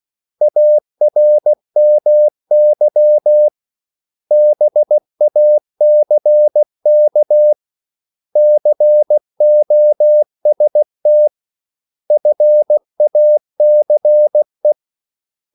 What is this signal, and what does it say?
Morse code, heard as a keyed tone, about 16 wpm, 600 Hz. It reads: ARMY BACK COST FACE
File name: CW / Practice / 500MostCommonEnglishWords16